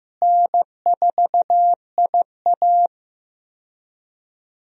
Morse code, heard as a keyed tone, about 15 wpm, 700 Hz. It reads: N4IA
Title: USA Call Signs - 4 Characters - Repeated 2x - 15wpm